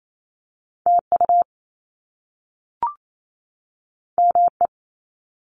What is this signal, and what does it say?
Morse code, heard as a keyed tone, about 28 wpm, 700 Hz. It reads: TU ME
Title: Top Two Letter Combinations in Words 28wpm